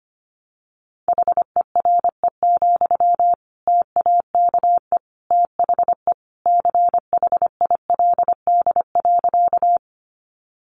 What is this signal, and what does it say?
Morse code, heard as a keyed tone, about 25 wpm, 700 Hz. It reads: HERE, TAKE THE CHILD.
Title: Sentences from Top 800 Words 25wpm